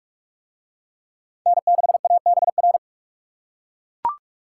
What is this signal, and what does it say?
Morse code, heard as a keyed tone, about 42 wpm, 700 Hz. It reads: N6ABR